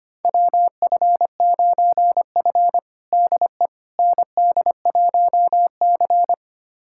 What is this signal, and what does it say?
Morse code, heard as a keyed tone, about 25 wpm, 700 Hz. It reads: WF9F DE ND1C
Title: USA Callsign DE Callsign - 4 Characters - 25wpm